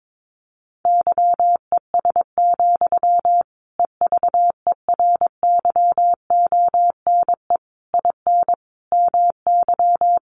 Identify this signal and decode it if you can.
Morse code, heard as a keyed tone, about 22 wpm, 700 Hz. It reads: YES, EVERYONE IN MY
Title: Sentences from Top 800 Words 22wpm